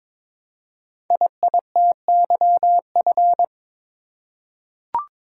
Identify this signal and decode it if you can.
Morse code, heard as a keyed tone, about 22 wpm, 700 Hz. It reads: IITYF
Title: Code Groups - 5 Letters - 22wpm